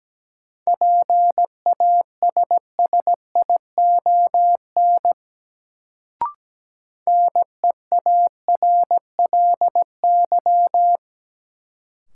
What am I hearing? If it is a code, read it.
Morse code, heard as a keyed tone, about 17 wpm, 700 Hz. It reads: PASSION NEARLY
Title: Sentences from Top 1300 Words - New Word Drill Alt - 17wpm